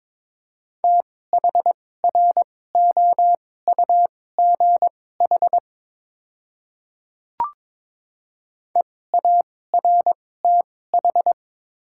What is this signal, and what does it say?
Morse code, heard as a keyed tone, about 21 wpm, spaced as for 16 wpm, 700 Hz. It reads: THROUGH EARTH
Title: 2x Character Spacing - Top 300 Words - 22wpm